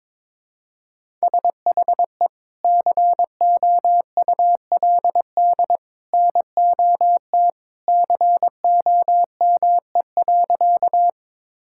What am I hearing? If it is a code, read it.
Morse code, heard as a keyed tone, about 22 wpm, 700 Hz. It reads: SHE COULD NOT COME.
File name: practice / 22wpm / Sentences with 2-Word Phrases - 22wpm